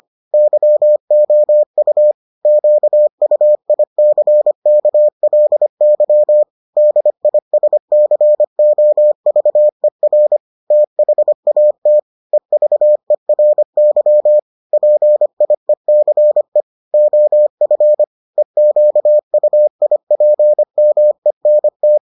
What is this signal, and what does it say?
Morse code, heard as a keyed tone, about 25 wpm, 600 Hz. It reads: YOU QUICKLY DISCOVER THAT EVERY PIECE OF EQUIPMENT